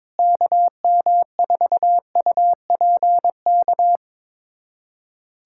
Morse code, heard as a keyed tone, about 22 wpm, 700 Hz. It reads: KM4UPK
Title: USA Callsign DE Callsign - 6 Characters - 22wpm